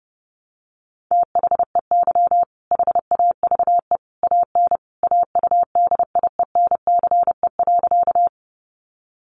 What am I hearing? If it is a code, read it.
Morse code, heard as a keyed tone, about 30 wpm, 700 Hz. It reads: THEY HAVE AN AUDIENCE.
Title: Sentences from Top 800 Words 30wpm